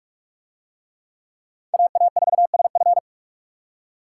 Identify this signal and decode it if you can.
Morse code, heard as a keyed tone, about 43 wpm, 700 Hz. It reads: AA4SF